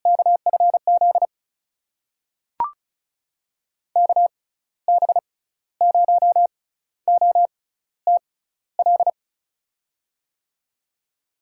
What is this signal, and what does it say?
Morse code, heard as a keyed tone, about 36 wpm, spaced as for 15 wpm, 700 Hz. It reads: KFZ KB0OTL
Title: USA Call Signs - 6 Characters - 6x Character Spacing - 35wpm